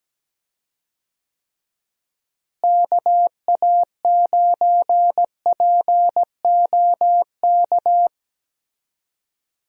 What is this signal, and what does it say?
Morse code, heard as a keyed tone, about 17 wpm, 700 Hz. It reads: KA9POK
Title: USA Call Signs - 6 Characters - Repeated 3x - 17wpm